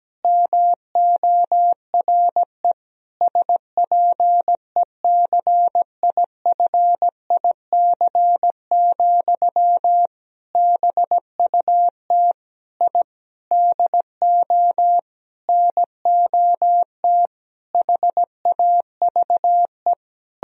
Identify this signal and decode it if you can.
Morse code, heard as a keyed tone, about 17 wpm, 700 Hz. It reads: MORE SPECIFIC, BUT I DO NOT HAVE
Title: Sentences from Top 1000 Words - Review Part 3 - 17wpm